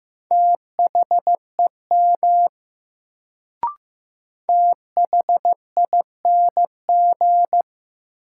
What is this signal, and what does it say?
Morse code, heard as a keyed tone, about 15 wpm, 700 Hz. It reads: THEM THING
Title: Sentences from Top 100 Words - New Word Drill - 15wpm